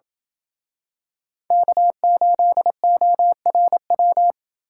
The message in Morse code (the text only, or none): K8ORW